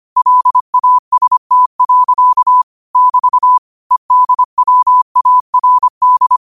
Morse code, heard as a keyed tone, about 25 wpm, 1000 Hz. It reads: LAST. <BT> EDWARD
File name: qotdfiles / 25wpm